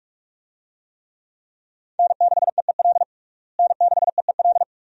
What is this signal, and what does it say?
Morse code, heard as a keyed tone, about 42 wpm, 700 Hz. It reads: N6EEL N6EEL
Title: USA Call Signs - 5 Characters - Repeated 2x - 45wpm